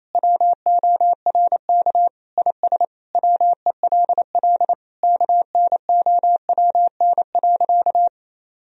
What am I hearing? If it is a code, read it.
Morse code, heard as a keyed tone, about 27 wpm, 700 Hz. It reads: WORK IS WELL KNOWN.
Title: Sentences from Top 1000 Words - Review Part 4 - 28wpm